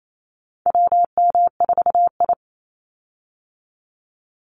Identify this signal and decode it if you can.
Morse code, heard as a keyed tone, about 28 wpm, 700 Hz. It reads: WM4I
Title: Call Signs Repeated 3x - 28wpm